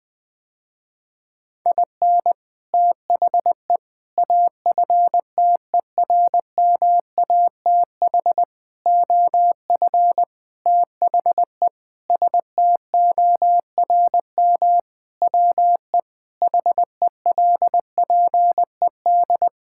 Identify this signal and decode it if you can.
Morse code, heard as a keyed tone, about 20 wpm, 700 Hz. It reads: IN THE AFTERMATH OF THE STORM WE HELPED
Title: Sentences with 5-Word Phrases - Spoken Beginning - 20wpm